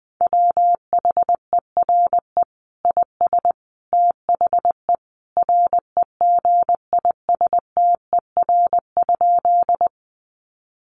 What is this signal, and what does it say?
Morse code, heard as a keyed tone, about 20 wpm, 700 Hz. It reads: WHERE IS THE REGISTER?
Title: Sentences from Top 1200 Words 20wpm